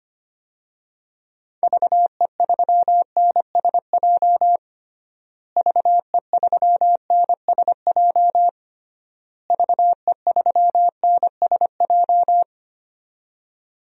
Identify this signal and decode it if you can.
Morse code, heard as a keyed tone, about 25 wpm, 700 Hz. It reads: VE3NSJ VE3NSJ VE3NSJ
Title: DXCC-100 Call Signs - 6 Characters - Repeated 3x - 25wpm